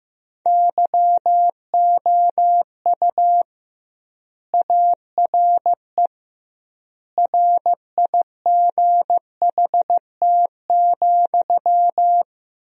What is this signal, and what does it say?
Morse code, heard as a keyed tone, about 15 wpm, 700 Hz. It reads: YOU ARE RIGHT,
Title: 2x Word-Spacing - Sentences from Top 400 Words - 15wpm